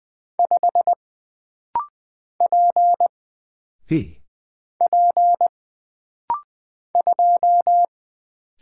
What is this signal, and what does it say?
Morse code, heard as a keyed tone, about 20 wpm, 700 Hz. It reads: 5 P P 2
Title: Single Character (Letter:Number:Punctuation:Prosign) - Rapid-Fire - 20wpm